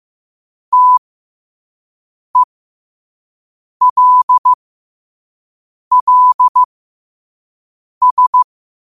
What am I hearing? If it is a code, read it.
Morse code, heard as a keyed tone, about 15 wpm, spaced as for 5 wpm, 1000 Hz. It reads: TELLS
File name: qotdfiles / 5wpm